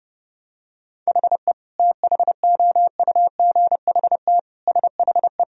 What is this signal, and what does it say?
Morse code, heard as a keyed tone, about 32 wpm, 700 Hz. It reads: HE THOUGHT SHE
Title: Sentences from Top 1000 Words - Review Part 1 - 30wpm